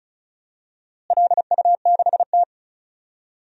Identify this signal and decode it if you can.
Morse code, heard as a keyed tone, about 34 wpm, 700 Hz. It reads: LU6T
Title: DXCC-100 Call Signs - 4 Characters - Repeated 2x - 35wpm